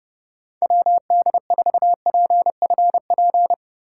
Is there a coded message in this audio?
Yes: Morse code from 0.6 to 3.5 s, heard as a keyed tone, about 29 wpm, 700 Hz. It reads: WD4PFP